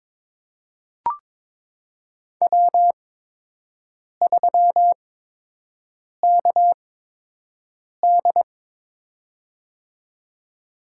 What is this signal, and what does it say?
Morse code, heard as a keyed tone, about 22 wpm, spaced as for 6 wpm, 700 Hz. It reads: W3KD
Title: USA Call Signs - 8x Character Spacing - 22wpm